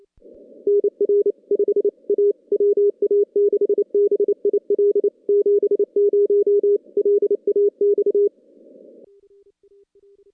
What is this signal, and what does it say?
Morse code, heard as a keyed tone, about 28 wpm, 410 Hz. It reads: NR 5 A WA6BIL 70 LAX
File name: contests / CW-SS-k5tr- -2100